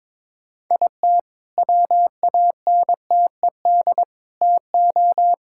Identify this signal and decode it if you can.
Morse code, heard as a keyed tone, about 22 wpm, 700 Hz. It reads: IT WANTED TO
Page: morse-code-ninja-practice-22wpm